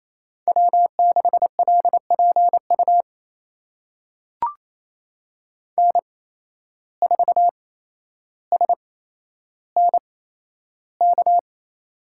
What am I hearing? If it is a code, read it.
Morse code, heard as a keyed tone, about 28 wpm, spaced as for 12 wpm, 700 Hz. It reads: W6LPU N4SNK
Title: USA Call Signs - 5 Characters - 8x Character Spacing - 28wpm